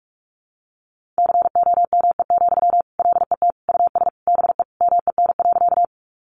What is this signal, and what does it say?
Morse code, heard as a keyed tone, about 44 wpm, 700 Hz. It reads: COME, LET US BE MEN.
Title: Sentences from Top 300 Words 45wpm